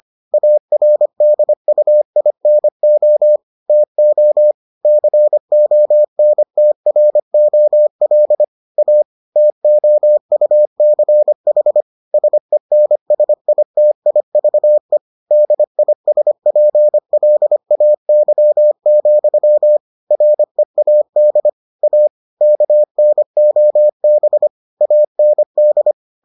Morse code, heard as a keyed tone, about 25 wpm, 600 Hz. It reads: ARDUINO TO CONTROL A TOUCH SENSITIVE DISPLAY, READ A KNOB AND